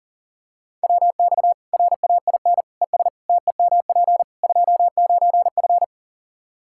Morse code, heard as a keyed tone, about 40 wpm, 700 Hz. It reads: WX RAIN ES TEMP 29F
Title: Practice QSO - Exchange 2 - Info - 40wpm